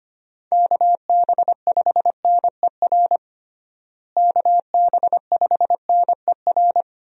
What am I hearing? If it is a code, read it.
Morse code, heard as a keyed tone, about 25 wpm, 700 Hz. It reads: KB5NER KB5NER
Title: USA Call Signs - 6 Characters - Repeated 2x - 25wpm